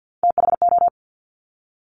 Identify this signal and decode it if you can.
Morse code, heard as a keyed tone, about 50 wpm, 700 Hz. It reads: THO
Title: Top Three Letter Combinations in Words 50wpm